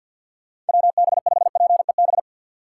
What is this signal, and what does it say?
Morse code, heard as a keyed tone, about 52 wpm, 700 Hz. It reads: WB5PEB
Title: USA Call Signs - 6 Characters - 4x Character Spacing - 50wpm